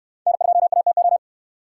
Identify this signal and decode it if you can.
Morse code, heard as a keyed tone, about 66 wpm, 700 Hz. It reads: N2ATO